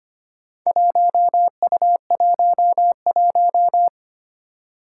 Morse code, heard as a keyed tone, about 25 wpm, 700 Hz. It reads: 1U11
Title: Four Challenging Letter-Number Combinations 25wpm